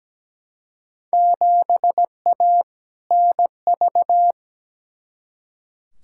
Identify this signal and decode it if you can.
Morse code, heard as a keyed tone, about 17 wpm, 700 Hz. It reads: 7A NV
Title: Field Day Exchanges Encore - 17wpm